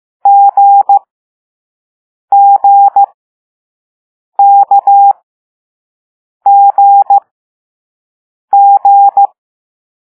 Morse code, heard as a keyed tone, about 15 wpm, spaced as for 5 wpm, 800 Hz. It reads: GGKGG
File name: Morse / Level4A